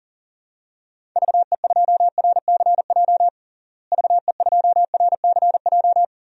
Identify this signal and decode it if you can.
Morse code, heard as a keyed tone, about 37 wpm, 700 Hz. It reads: VE2RCJ VE2RCJ
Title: DXCC-100 Call Signs - 6 Characters - Repeated 2x - 40wpm